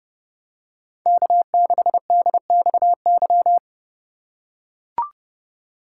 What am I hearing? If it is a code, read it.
Morse code, heard as a keyed tone, about 29 wpm, 700 Hz. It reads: K6DXY